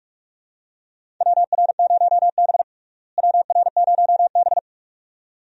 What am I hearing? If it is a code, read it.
Morse code, heard as a keyed tone, about 47 wpm, 700 Hz. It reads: WR0B WR0B
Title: USA Call Signs - 4 Characters - Repeated 2x - 45wpm